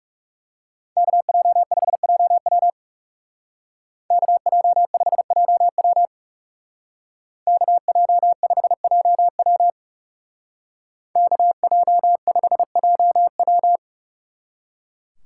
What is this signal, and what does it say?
Morse code, heard as a keyed tone, about 39 wpm, 700 Hz. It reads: KJ5JW KJ5JW KJ5JW KJ5JW